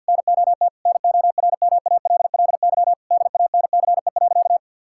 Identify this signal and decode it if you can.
Morse code, heard as a keyed tone, about 50 wpm, 700 Hz. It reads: NOT NORMALLY DANCE.
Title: Sentences from Top 1000 Words - Review Part 6 - 50wpm